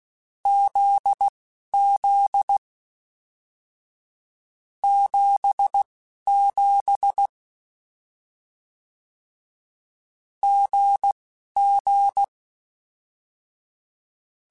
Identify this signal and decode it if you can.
Morse code, heard as a keyed tone, about 16 wpm, spaced as for 11 wpm, 800 Hz. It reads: ZZ 77 GG
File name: Difficult Combos / Difficult Combos-mp3s / GZ GZ7 @10